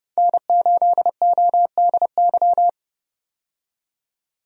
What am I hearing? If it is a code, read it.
Morse code, heard as a keyed tone, about 32 wpm, 700 Hz. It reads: N8ODY
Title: USA Call Signs - 5 Characters - 30wpm